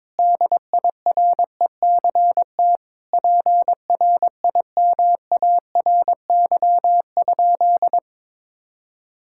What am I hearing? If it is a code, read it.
Morse code, heard as a keyed tone, about 22 wpm, 700 Hz. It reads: DIRECT PRIMARY?
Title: Sentences from Top 1000 Words - Review Part 5 - 22wpm